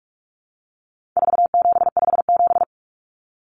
Morse code, heard as a keyed tone, about 44 wpm, 700 Hz. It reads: 4757
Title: Numbers - 4-Digit 45wpm